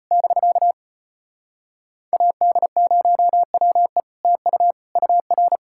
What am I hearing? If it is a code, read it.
Morse code, heard as a keyed tone, about 33 wpm, 700 Hz. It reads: <BK> AD0WE TU UR